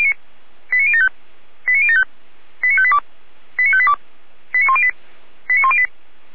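Five-tone selective call (ZVEI), heard as quick runs of five tones, tones of about 70 ms. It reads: 89075 89075 89582 89582 89108 89108
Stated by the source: ZVEI five-tone call 89013 received by SDR.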